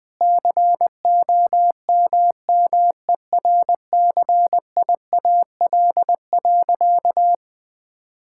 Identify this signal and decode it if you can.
Morse code, heard as a keyed tone, about 20 wpm, 700 Hz. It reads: COMMERCIAL.
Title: Sentences from Top 1000 Words - Review Part 5 - 20wpm